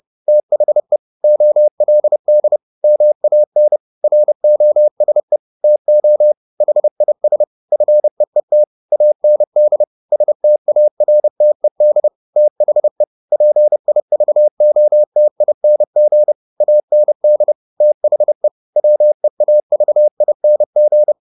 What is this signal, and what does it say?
Morse code, heard as a keyed tone, about 30 wpm, 600 Hz. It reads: THE OLD MAN ROSE TO HIS FEET AND STARTED THE PIVOTING AND THE WEAVING